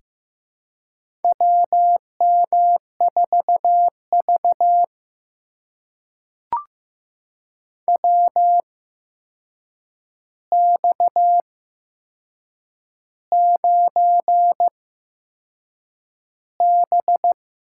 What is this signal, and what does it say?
Morse code, heard as a keyed tone, about 15 wpm, spaced as for 6 wpm, 700 Hz. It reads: WM4V WX9B
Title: USA Call Signs - 4 Characters - 8x Character Spacing - 15wpm